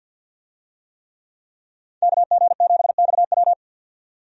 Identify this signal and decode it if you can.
Morse code, heard as a keyed tone, about 47 wpm, 700 Hz. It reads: KG7XW
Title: USA Call Signs - 5 Characters - 50wpm